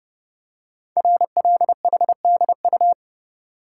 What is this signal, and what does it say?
Morse code, heard as a keyed tone, about 30 wpm, 700 Hz. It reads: RLHDU
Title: Code Groups - 5 Letters - 30wpm